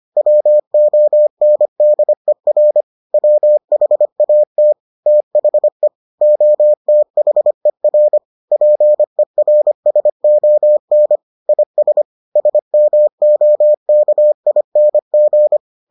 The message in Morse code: WONDER WHAT THE OTHER PERSON IS SMOKING

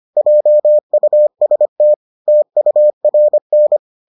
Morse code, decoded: JUST TURN

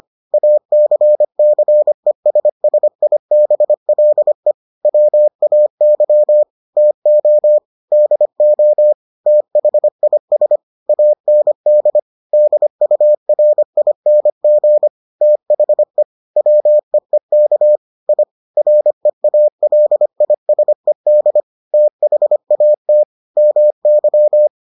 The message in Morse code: ACCESSIBLE WAY TO DO THIS AND DURING THE WEEK I REALISED THAT MY

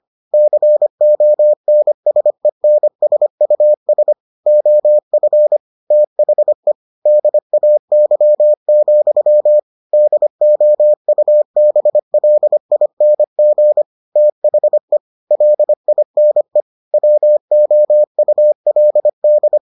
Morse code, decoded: CONSENSUS OF THE DAY, DOUBLING THE LINE WOULD

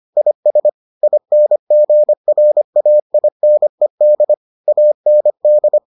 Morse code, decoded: IS INGRAINED AND

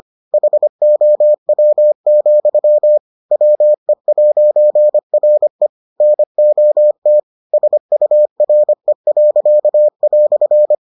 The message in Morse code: HOW, WE'RE NOT SURE."

